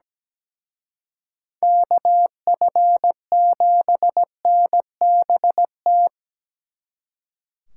Morse code, heard as a keyed tone, about 19 wpm, 700 Hz. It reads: KF7NBT
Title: Speed Racing - USA 2x3 Call Signs 17wpm